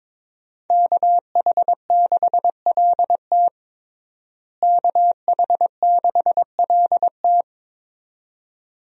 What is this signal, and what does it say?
Morse code, heard as a keyed tone, about 22 wpm, 700 Hz. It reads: KH6LT KH6LT
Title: USA Call Signs - 5 Characters - Repeated 2x - 22wpm